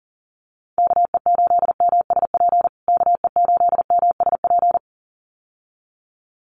Morse code, heard as a keyed tone, about 40 wpm, 700 Hz. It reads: KE8MSP KE8MSP
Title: USA Call Signs Repeated 2x 40wpm